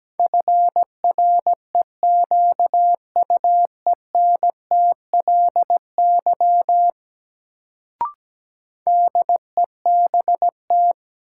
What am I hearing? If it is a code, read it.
Morse code, heard as a keyed tone, about 17 wpm, 700 Hz. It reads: FREQUENTLY DEBT
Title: Sentences from Top 1300 Words - New Word Drill Alt - 17wpm